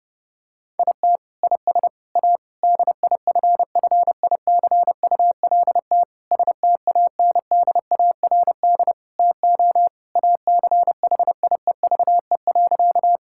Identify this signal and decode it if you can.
Morse code, heard as a keyed tone, about 30 wpm, 700 Hz. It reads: IT IS A DIFFICULT STANDARD TO ACHIEVE.
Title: Sentences from Top 1000 Words - Review Part 6 - 30wpm